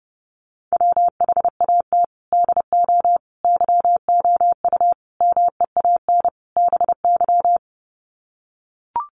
Morse code, heard as a keyed tone, about 30 wpm, 700 Hz. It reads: WHAT DO YOU MEAN BY
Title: Sets of 5 Words 30wpm